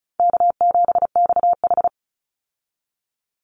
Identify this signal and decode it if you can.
Morse code, heard as a keyed tone, about 34 wpm, 700 Hz. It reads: K7XH
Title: Call Signs Repeated 3x - 35wpm